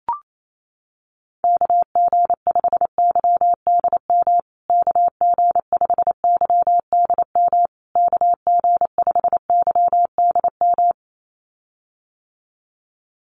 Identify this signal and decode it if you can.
Morse code, heard as a keyed tone, about 28 wpm, 700 Hz. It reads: KG5YDM KG5YDM KG5YDM
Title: USA Call Signs Repeated 3x 28wpm